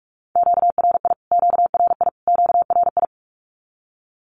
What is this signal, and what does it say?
Morse code, heard as a keyed tone, about 44 wpm, 700 Hz. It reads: QRI QRI QRI
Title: Q-Codes Repeated 3x 45wpm